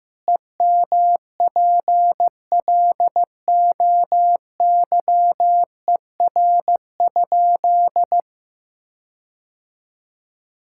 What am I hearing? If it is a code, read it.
Morse code, heard as a keyed tone, about 15 wpm, 700 Hz. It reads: EMPLOYER?
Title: Sentences from Top 1600 Words 15wpm